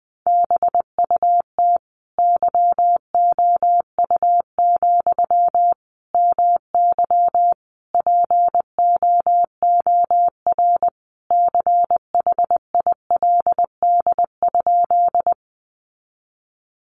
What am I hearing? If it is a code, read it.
Morse code, heard as a keyed tone, about 19 wpm, 700 Hz. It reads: BUT YOU, MY POOR CHILD?
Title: Sentences from Top 1000 Words 20wpm